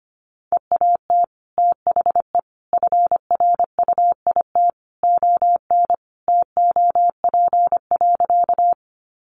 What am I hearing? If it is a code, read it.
Morse code, heard as a keyed tone, about 25 wpm, 700 Hz. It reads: EAT THE FRUIT ON TOP.
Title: Sentences from Top 1300 Words 25wpm